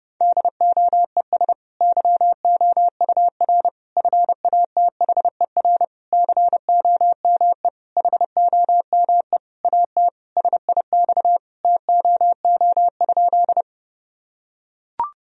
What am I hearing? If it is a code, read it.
Morse code, heard as a keyed tone, about 30 wpm, 700 Hz. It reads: DOES YOUR FATHER COME HOME AT SIX TOO?
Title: Sentences from Top 1000 Words - Review Part 2 - 30wpm